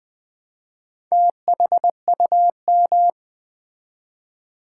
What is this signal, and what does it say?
Morse code, heard as a keyed tone, about 20 wpm, 700 Hz. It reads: THUM
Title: Build A Word - 1 Letter At A Time - 5-Letter Words - 20wpm